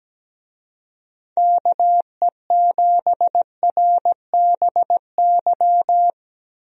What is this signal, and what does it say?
Morse code, heard as a keyed tone, about 17 wpm, 700 Hz. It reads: KE7RBY
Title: USA Call Signs - 6 Characters - 17wpm